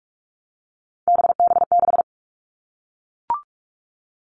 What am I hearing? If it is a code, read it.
Morse code, heard as a keyed tone, about 44 wpm, 700 Hz. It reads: BB6
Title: All Cut Numbers - 3-Digit 45wpm